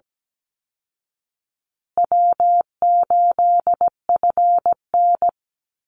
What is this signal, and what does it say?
Morse code, heard as a keyed tone, about 17 wpm, 700 Hz. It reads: W8FN
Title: Easy Call Signs Repeated 3x - 17wpm